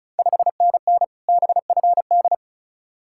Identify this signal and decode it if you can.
Morse code, heard as a keyed tone, about 36 wpm, 690 Hz. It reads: 5NN BFD
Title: Ontario QSO Party Exchange 35wpm